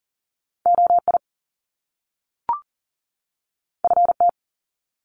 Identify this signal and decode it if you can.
Morse code, heard as a keyed tone, about 40 wpm, 700 Hz. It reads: OI FT
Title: Top Two Letter Combinations in Words 40wpm